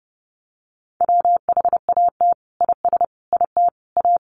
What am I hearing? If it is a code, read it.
Morse code, heard as a keyed tone, about 30 wpm, 700 Hz. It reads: WHAT IS IT A
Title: Sentences from Top 1300 Words 30wpm